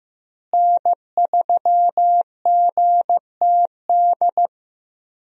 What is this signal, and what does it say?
Morse code, heard as a keyed tone, about 15 wpm, 700 Hz. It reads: N3GTD
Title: USA Call Signs - 5 Characters - Repeated 2x - 15wpm